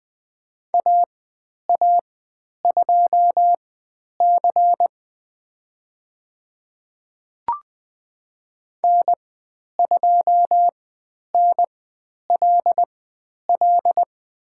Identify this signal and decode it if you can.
Morse code, heard as a keyed tone, about 20 wpm, spaced as for 10 wpm, 700 Hz. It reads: AA2C N2NLL
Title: 109 Lesson 18g - Letter m - Call Signs 20wpm